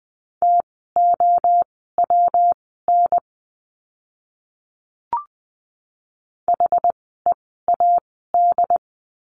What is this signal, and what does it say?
Morse code, heard as a keyed tone, about 20 wpm, spaced as for 15 wpm, 700 Hz. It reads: TOWN HEAD
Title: 115 Lesson 19f - Letter w - Words 20wpm